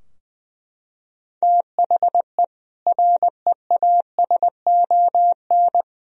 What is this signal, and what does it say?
Morse code, heard as a keyed tone, about 20 wpm, 700 Hz. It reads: THE REASON
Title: Sentences from Top 1000 Words - Review Part 2 - 20wpm